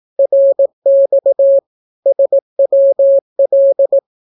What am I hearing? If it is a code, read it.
Morse code, heard as a keyed tone, about 18 wpm, 550 Hz. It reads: RX SWL
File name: naukapodstaw / skroty / skroty